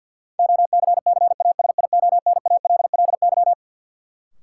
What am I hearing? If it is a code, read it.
Morse code, heard as a keyed tone, about 48 wpm, 700 Hz. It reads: OCCASIONALLY
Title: Sentences from Top 1300 Words - New Word Drill - 50wpm